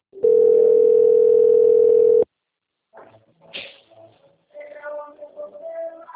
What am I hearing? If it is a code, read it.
Ringback tone (North America), 440 + 480 Hz.
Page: Sao Carlos